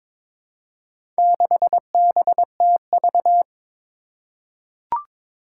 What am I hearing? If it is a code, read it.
Morse code, heard as a keyed tone, about 22 wpm, 700 Hz. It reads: 6BTV
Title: Four Challenging Letter-Number Combinations 22wpm